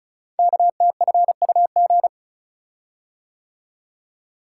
Morse code, heard as a keyed tone, about 33 wpm, 700 Hz. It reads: KTFUG
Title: Code Groups - 5 Letters - 35wpm